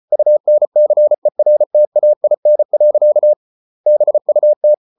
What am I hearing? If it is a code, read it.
Morse code, heard as a keyed tone, about 34 wpm, 600 Hz. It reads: UNCERTAIN. BUT